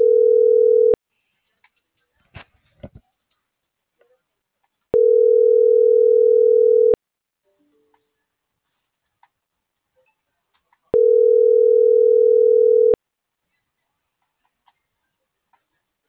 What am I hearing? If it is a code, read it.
Ringback tone (North America), 440 + 480 Hz.